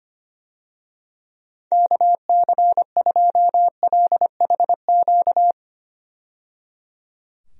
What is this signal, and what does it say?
Morse code, heard as a keyed tone, about 25 wpm, 700 Hz. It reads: KC2LHQ
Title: USA Call Signs - 6 Characters - 25wpm